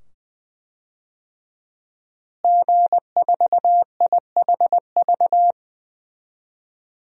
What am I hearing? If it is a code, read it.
Morse code, heard as a keyed tone, about 20 wpm, 700 Hz. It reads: G4IHV
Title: DXCC-100 Call Signs - 5 Characters - Repeated 2x - 20wpm